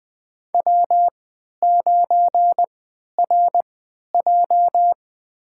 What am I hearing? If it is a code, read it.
Morse code, heard as a keyed tone, about 20 wpm, spaced as for 11 wpm, 700 Hz. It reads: W9RJ
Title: USA Call Signs - 4 Characters - 3x Character Spacing - 20wpm